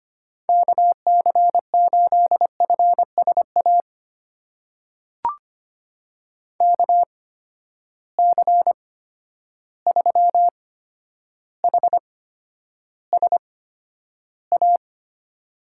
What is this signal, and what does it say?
Morse code, heard as a keyed tone, about 25 wpm, spaced as for 11 wpm, 700 Hz. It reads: KC8FSA KC3HSA
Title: USA Call Signs - 6 Characters - 8x Character Spacing - 25wpm